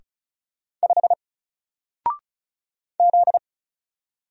Morse code, heard as a keyed tone, about 34 wpm, 700 Hz. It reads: 5 Z
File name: Single-Letter-Number-35wpm